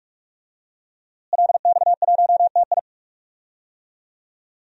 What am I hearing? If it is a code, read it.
Morse code, heard as a keyed tone, about 45 wpm, 700 Hz. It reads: LX1TI
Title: DXCC-100 Call Signs - 5 Characters - Repeated 3x - 45wpm